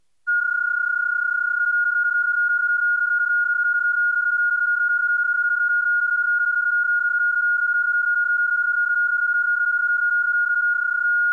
PSK31, about 31.25 baud.